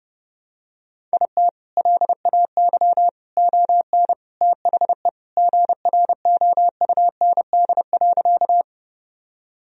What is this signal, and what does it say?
Morse code, heard as a keyed tone, about 32 wpm, 700 Hz. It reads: IT LAY ON THE GROUND.
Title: Sentences from Top 1000 Words - Review Part 2 - 30wpm